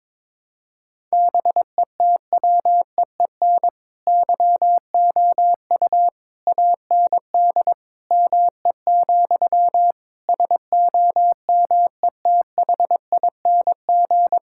Morse code, heard as a keyed tone, about 22 wpm, 700 Hz. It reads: BETWEEN YOU AND ME, SOMETHING
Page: morse-code-ninja-practice-22wpm